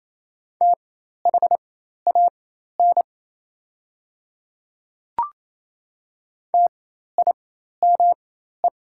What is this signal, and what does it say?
Morse code, heard as a keyed tone, about 27 wpm, spaced as for 13 wpm, 700 Hz. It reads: THAN TIME
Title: 4x Character Spacing - Top 100 Words - 28wpm